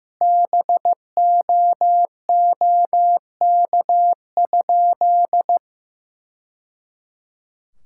Morse code, heard as a keyed tone, about 15 wpm, 700 Hz. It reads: BOOK?
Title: Sentences from Top 1000 Words - Review Part 6 - 15wpm